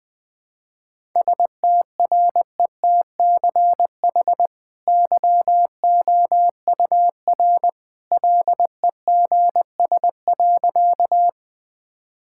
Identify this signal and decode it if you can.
Morse code, heard as a keyed tone, about 20 wpm, 700 Hz. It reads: STRETCH YOUR LEGS.
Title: Easy Sentences with Hints - 20wpm